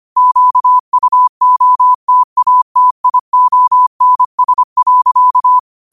Morse code, heard as a keyed tone, about 25 wpm, 1000 Hz. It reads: QUOTATIONS.
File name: qotdfiles / 25wpm